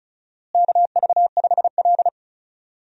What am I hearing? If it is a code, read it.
Morse code, heard as a keyed tone, about 36 wpm, 700 Hz. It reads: KV5L
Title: USA Call Signs - 4 Characters - Repeated 3x - 35wpm